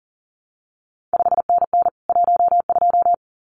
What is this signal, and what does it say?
Morse code, heard as a keyed tone, about 40 wpm, 700 Hz. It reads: 5NN 12
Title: CQ WW DX Contest Exchange Encore - 40wpm